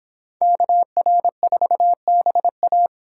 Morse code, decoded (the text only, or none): KR4BA